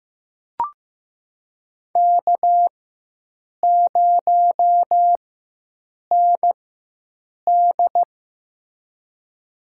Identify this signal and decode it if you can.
Morse code, heard as a keyed tone, about 15 wpm, spaced as for 7 wpm, 700 Hz. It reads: K0ND
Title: USA Call Signs - 4 Characters - 4x Character Spacing - 15wpm